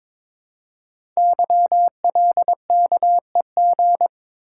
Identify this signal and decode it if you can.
Morse code, heard as a keyed tone, about 22 wpm, 700 Hz. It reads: YLKEG
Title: Code Groups - 5 Letters - 22wpm